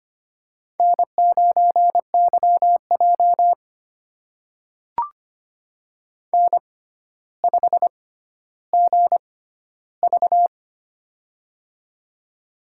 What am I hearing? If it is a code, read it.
Morse code, heard as a keyed tone, about 25 wpm, spaced as for 13 wpm, 700 Hz. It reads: N9YJ N5GV
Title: USA Call Signs - 4 Characters - 6x Character Spacing - 25wpm